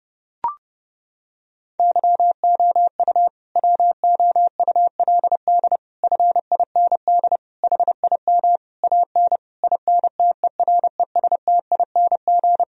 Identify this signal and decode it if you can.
Morse code, heard as a keyed tone, about 30 wpm, 700 Hz. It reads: YOU WOULD FIND HIM AN INTERESTING
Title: Sentences from Top 1000 Words - Review Part 1 - 30wpm